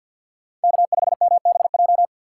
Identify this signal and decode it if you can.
Morse code, heard as a keyed tone, about 53 wpm, 700 Hz. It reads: K5MBJ
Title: USA Call Signs - 5 Characters - 2x Character Spacing - 50wpm